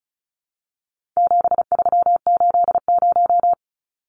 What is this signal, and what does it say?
Morse code, heard as a keyed tone, about 34 wpm, 700 Hz. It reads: 7380
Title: Numbers - 4-Digit 35wpm